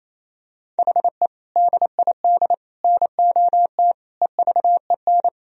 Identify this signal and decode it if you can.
Morse code, heard as a keyed tone, about 28 wpm, 700 Hz. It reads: HE DID NOT EVEN
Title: Sentences from Top 1000 Words - Review Part 1 - 28wpm